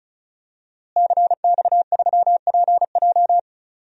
Morse code, heard as a keyed tone, about 34 wpm, 700 Hz. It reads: CX3PJ
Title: DXCC-100 Call Signs - 5 Characters - Repeated 3x - 35wpm